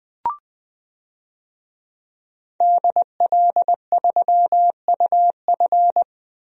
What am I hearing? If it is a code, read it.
Morse code, heard as a keyed tone, about 20 wpm, 700 Hz. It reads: DL3UF
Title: DXCC-100 Call Signs - 5 Characters - 20wpm